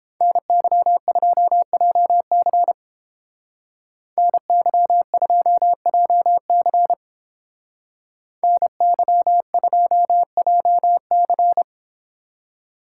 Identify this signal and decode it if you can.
Morse code, heard as a keyed tone, about 30 wpm, 700 Hz. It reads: NY2JC NY2JC NY2JC